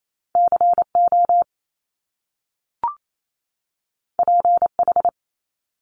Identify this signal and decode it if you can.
Morse code, heard as a keyed tone, about 28 wpm, 700 Hz. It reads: CO PH
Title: Top Two Letter Combinations in Words 28wpm